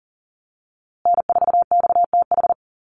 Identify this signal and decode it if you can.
Morse code, heard as a keyed tone, about 40 wpm, 700 Hz. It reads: N4XTH
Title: USA Call Signs Repeated 3x 40wpm